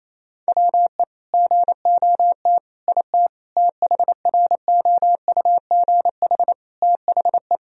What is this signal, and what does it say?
Morse code, heard as a keyed tone, about 28 wpm, 700 Hz. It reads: WE GOT IT THROUGH THE
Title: Sentences from Top 1000 Words - Review Part 1 - 28wpm